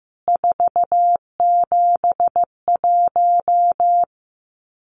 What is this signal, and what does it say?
Morse code, heard as a keyed tone, about 14 wpm, 700 Hz. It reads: 471
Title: CWT Exchange 15wpm